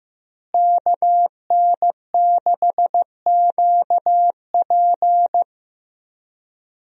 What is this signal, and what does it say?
Morse code, heard as a keyed tone, about 16 wpm, 700 Hz. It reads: KN6QP
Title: USA Call Signs - 5 Characters - Repeated 3x - 15wpm